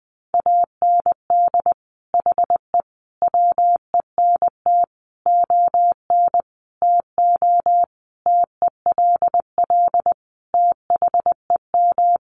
Morse code, heard as a keyed tone, about 20 wpm, 700 Hz. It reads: AND HE WENT ON TO TELL THEM
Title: Sentences from Top 900 Words 20wpm